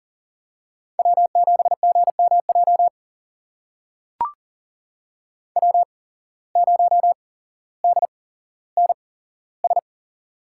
Morse code, heard as a keyed tone, about 40 wpm, spaced as for 17 wpm, 700 Hz. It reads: W7GMJ W0DNS
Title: USA Call Signs - 8x Character Spacing - 40wpm